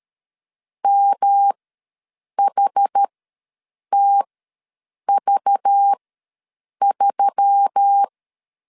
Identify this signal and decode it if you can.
Morse code, heard as a keyed tone, about 13 wpm, spaced as for 7 wpm, 800 Hz. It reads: MHTV3
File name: LEZIONI / 6 PKX0 / 6 P ws 12wpm 3117